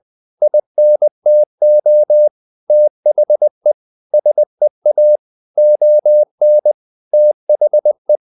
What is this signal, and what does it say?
Morse code, heard as a keyed tone, about 20 wpm, 600 Hz. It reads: INTO THE SEA ON THE
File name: the old man and the sea part 2 20wpm